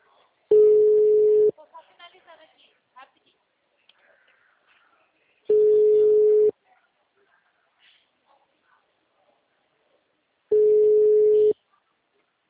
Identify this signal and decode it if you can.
Ringback tone (Europe, 425 Hz), with speech.
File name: out- -2020